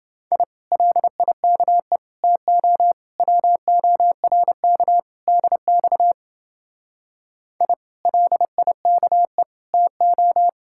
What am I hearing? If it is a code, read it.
Morse code, heard as a keyed tone, about 29 wpm, 700 Hz. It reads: I LIKE TO WORK DX I LIKE TO